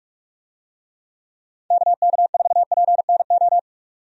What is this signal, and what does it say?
Morse code, heard as a keyed tone, about 46 wpm, 700 Hz. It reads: KK4PNO
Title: USA Call Signs - 6 Characters - 45wpm